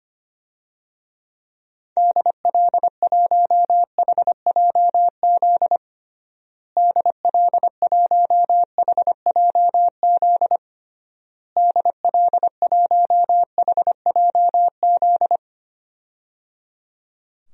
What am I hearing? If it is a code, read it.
Morse code, heard as a keyed tone, about 25 wpm, 700 Hz. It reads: DL1HJZ DL1HJZ DL1HJZ
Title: DXCC-100 Call Signs - 6 Characters - Repeated 3x - 25wpm